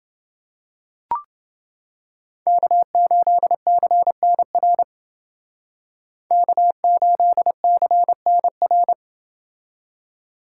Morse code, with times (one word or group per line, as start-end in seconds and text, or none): 2.5-4.8 K8CNR
6.3-8.9 K8CNR